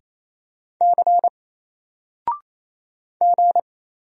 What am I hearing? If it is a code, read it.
Morse code, heard as a keyed tone, about 28 wpm, 700 Hz. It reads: C G
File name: Single-Letter-Number-28wpm